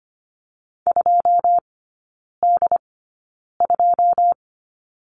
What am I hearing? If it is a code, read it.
Morse code, heard as a keyed tone, about 25 wpm, spaced as for 9 wpm, 700 Hz. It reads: 2D2
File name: Lesson 14e - Number 2 - Three Characters 25wpm